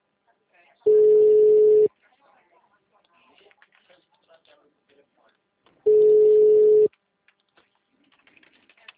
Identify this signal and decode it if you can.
Ringback tone (Europe, 425 Hz), with speech.